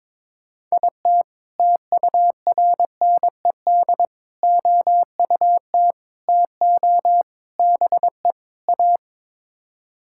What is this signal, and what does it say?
Morse code, heard as a keyed tone, about 22 wpm, 700 Hz. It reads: IT TURNED OUT TO BE A
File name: practice / 22wpm / Sets of 6 Words - 22wpm